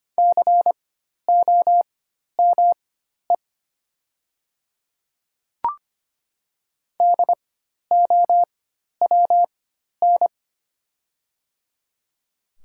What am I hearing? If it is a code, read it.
Morse code, heard as a keyed tone, about 25 wpm, spaced as for 12 wpm, 700 Hz. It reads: COME DOWN
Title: 4x Character Spacing - Top 100 Words - 25wpm